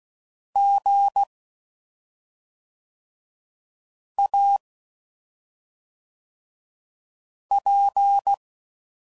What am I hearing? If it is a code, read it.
Morse code, heard as a keyed tone, about 16 wpm, 800 Hz. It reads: G A P
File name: Suffixes / Special ING's / SlowToFaster ING's